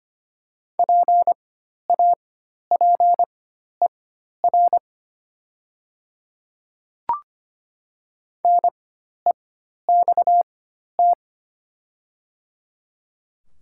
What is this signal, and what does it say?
Morse code, heard as a keyed tone, about 25 wpm, spaced as for 12 wpm, 700 Hz. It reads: PAPER NEXT
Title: 4x Character Spacing - Top 300 Words - 25wpm